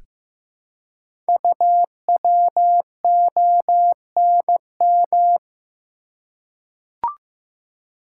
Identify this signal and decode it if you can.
Morse code, heard as a keyed tone, about 15 wpm, 700 Hz. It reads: UWONM
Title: Code Groups - 5 Letters - 15wpm